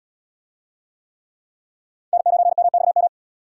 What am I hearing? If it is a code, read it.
Morse code, heard as a keyed tone, about 70 wpm, 700 Hz. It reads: N9MXM